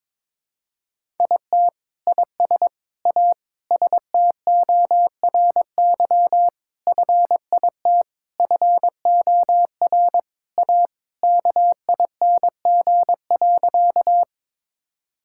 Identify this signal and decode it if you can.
Morse code, heard as a keyed tone, about 22 wpm, 700 Hz. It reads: IT IS A STORY FIT FOR A KING.
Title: Sentences from Top 1000 Words - Review Part 5 - 22wpm